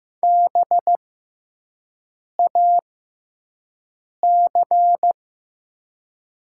Morse code, heard as a keyed tone, about 15 wpm, spaced as for 5 wpm, 700 Hz. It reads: BAC
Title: 6x Character Spacing - Top 500 Words - 15wpm